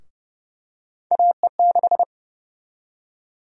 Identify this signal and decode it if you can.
Morse code, heard as a keyed tone, about 30 wpm, 700 Hz. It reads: AE6
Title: USA Call Sign Prefixes - Repeated 2x - 30wpm